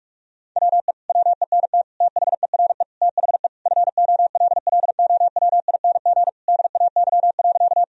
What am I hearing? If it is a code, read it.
Morse code, heard as a keyed tone, about 45 wpm, 700 Hz. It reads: WE WENT THERE THE FOLLOWING DAY.
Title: Sentences from Top 1000 Words - Review Part 4 - 45wpm